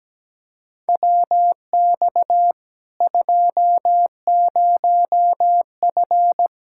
Morse code, heard as a keyed tone, about 17 wpm, 700 Hz. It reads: WX 20F
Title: Practice QSO - Exchange 2 - Info - 17wpm